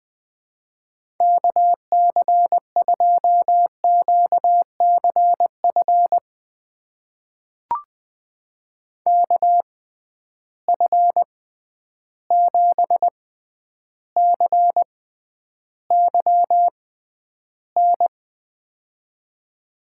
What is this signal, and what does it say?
Morse code, heard as a keyed tone, about 20 wpm, spaced as for 10 wpm, 700 Hz. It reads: KC2QCF KF7CYN
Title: USA Call Signs - 6x Character Spacing - 20wpm